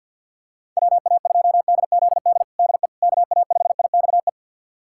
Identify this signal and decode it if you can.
Morse code, heard as a keyed tone, about 49 wpm, 700 Hz. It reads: WA2DZD DE KA5IXE
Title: USA Callsign DE Callsign - 6 Characters - 50wpm